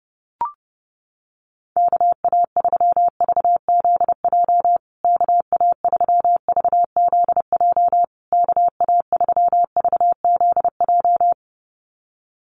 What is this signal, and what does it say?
Morse code, heard as a keyed tone, about 30 wpm, 700 Hz. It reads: KA3VZJ KA3VZJ KA3VZJ
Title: USA Call Signs Repeated 3x 30wpm